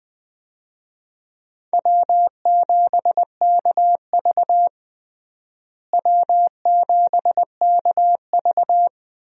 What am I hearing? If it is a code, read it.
Morse code, heard as a keyed tone, about 20 wpm, 700 Hz. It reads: W7KV W7KV
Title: USA Call Signs - 4 Characters - Repeated 2x - 20wpm